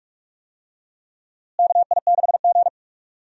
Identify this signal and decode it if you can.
Morse code, heard as a keyed tone, about 45 wpm, 700 Hz. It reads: KI6G
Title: USA Call Signs - 4 Characters - Repeated 3x - 45wpm